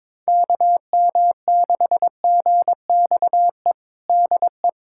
Morse code, heard as a keyed tone, about 22 wpm, 700 Hz. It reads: KM6GXE DE
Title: USA Callsign DE Callsign - 6 Characters - 22wpm